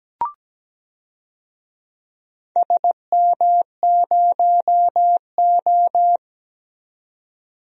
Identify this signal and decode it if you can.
Morse code, heard as a keyed tone, about 17 wpm, 700 Hz. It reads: SM0O
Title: DXCC-100 Call Signs - 4 Characters - 17wpm